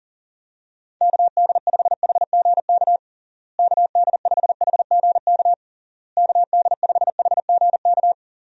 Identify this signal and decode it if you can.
Morse code, heard as a keyed tone, about 40 wpm, 700 Hz. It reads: KD5HGK KD5HGK KD5HGK
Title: USA Call Signs - 6 Characters - Repeated 3x - 40wpm